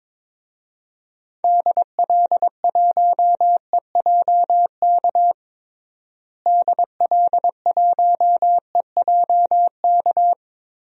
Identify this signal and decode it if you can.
Morse code, heard as a keyed tone, about 22 wpm, 700 Hz. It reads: DL1EJK DL1EJK
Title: DXCC-100 Call Signs - 6 Characters - Repeated 3x - 22wpm